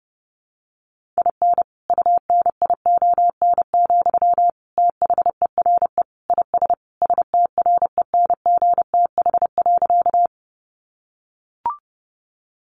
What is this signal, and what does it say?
Morse code, heard as a keyed tone, about 30 wpm, 700 Hz. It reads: IN UNION, THERE IS STRENGTH.
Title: Sentences from Top 1400 Words 30wpm